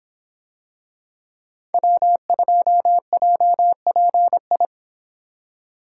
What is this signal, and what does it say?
Morse code, heard as a keyed tone, about 25 wpm, 700 Hz. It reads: W2JPI